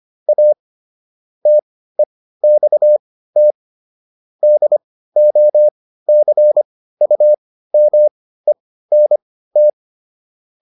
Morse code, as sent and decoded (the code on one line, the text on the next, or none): .- / - . -..- - / -.. --- -.-. ..- -- . -. -
A TEXT DOCUMENT